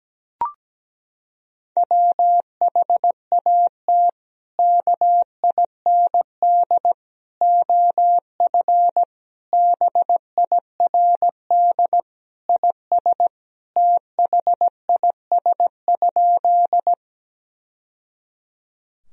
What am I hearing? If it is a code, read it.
Morse code, heard as a keyed tone, about 17 wpm, 700 Hz. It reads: WHAT KIND OF BIRD IS THIS?
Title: Sentences from Top 1000 Words - Review Part 2 - 17wpm